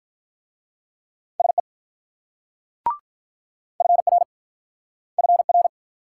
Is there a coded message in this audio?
Yes: Morse code from 1.4 to 5.7 s, heard as a keyed tone, about 46 wpm, 700 Hz. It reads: SE FR FR